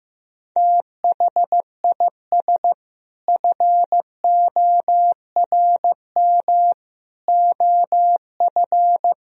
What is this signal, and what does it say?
Morse code, heard as a keyed tone, about 15 wpm, 700 Hz. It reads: THIS FORM OF
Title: Sentences from Top 1000 Words - Review Part 4 - 15wpm